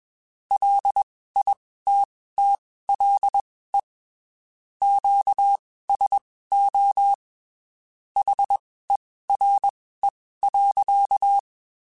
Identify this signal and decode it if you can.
Morse code, heard as a keyed tone, about 21 wpm, spaced as for 15 wpm, 800 Hz. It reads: LITTLE QSO HERE.
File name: (spaced) / QSOs Spaced-mp3s / WD0FBT @20 sp